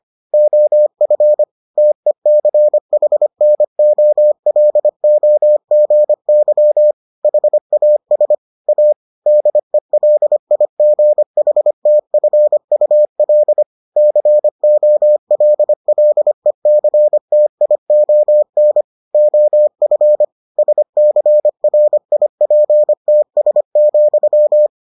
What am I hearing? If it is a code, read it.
Morse code, heard as a keyed tone, about 25 wpm, 600 Hz. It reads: OF TECHNOLOGY HAS A DELIGHTFUL COLLECTION OF SCRIPTS,